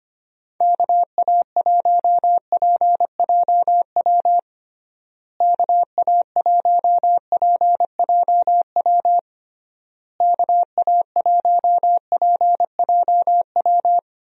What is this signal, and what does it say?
Morse code, heard as a keyed tone, about 25 wpm, 700 Hz. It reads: KA1PJW KA1PJW KA1PJW
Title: USA Call Signs - 6 Characters - Repeated 3x - 25wpm